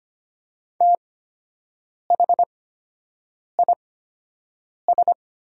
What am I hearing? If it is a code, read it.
Morse code, heard as a keyed tone, about 25 wpm, spaced as for 7 wpm, 700 Hz. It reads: THIS
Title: 8x Character Spacing - Top 100 Words - 25wpm